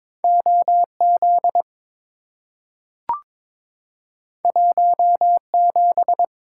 Morse code, heard as a keyed tone, about 22 wpm, 700 Hz. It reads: OZ 17
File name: practice / 22wpm / Two Letter-Number Combinations - 22wpm